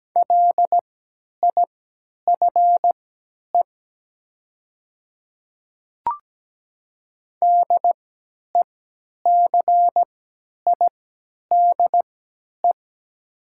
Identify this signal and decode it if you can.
Morse code, heard as a keyed tone, about 17 wpm, spaced as for 10 wpm, 700 Hz. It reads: LIFE DECIDE
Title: 3x Character Spacing - Top 500 Words - 17wpm